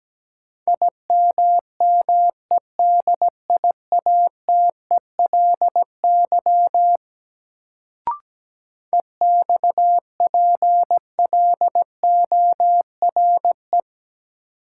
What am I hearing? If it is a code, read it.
Morse code, heard as a keyed tone, about 17 wpm, 700 Hz. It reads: IMMEDIATELY EXPLORE
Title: Sentences from Top 1100 Words - New Word Drill Alt - 17wpm